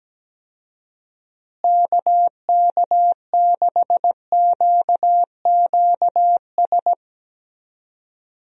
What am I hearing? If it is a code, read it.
Morse code, heard as a keyed tone, about 17 wpm, 700 Hz. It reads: KK6QQS
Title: USA Call Signs - 6 Characters - Repeated 2x - 17wpm